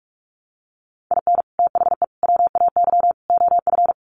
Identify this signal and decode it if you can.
Morse code, heard as a keyed tone, about 44 wpm, 700 Hz. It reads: IN THE WAY OF
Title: Sets of 4 Words 45wpm